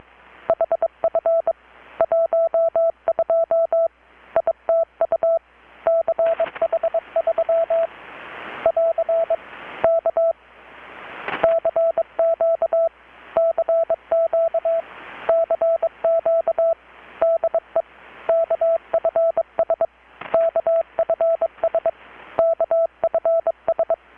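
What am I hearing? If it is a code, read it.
Morse code, heard as a keyed tone, about 22 wpm, 660 Hz. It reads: HF 12 ITU CH3 <AR> K CQ CQ CQ DE KFS KFS KFS
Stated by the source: Morse from KPH's Night of Nights XX maritime radiotelegraphy event, received in Chiapas, Mexico about 3,800 km away on a Kenwood R-600 with a grounded 10 m long-wire antenna.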